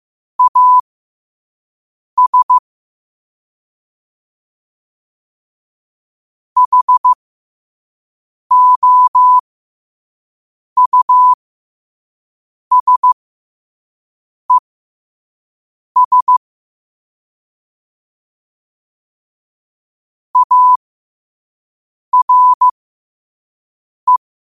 Morse code, heard as a keyed tone, about 15 wpm, spaced as for 5 wpm, 1000 Hz. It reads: AS HOUSES ARE
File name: qotdfiles / 5wpm